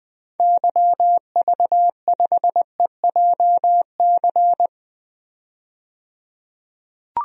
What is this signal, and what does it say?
Morse code, heard as a keyed tone, about 20 wpm, 700 Hz. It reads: YV5EJC
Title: DXCC-100 Call Signs - 6 Characters - 20wpm